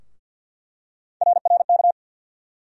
Morse code, heard as a keyed tone, about 47 wpm, 700 Hz. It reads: RRK
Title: USA Call Sign Postfixes - 3 Letters 50wpm